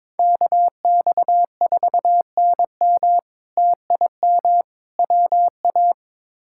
Morse code, heard as a keyed tone, about 22 wpm, 700 Hz. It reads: KX4NM TIM WA
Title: K1USN SST QSO Practice - 22wpm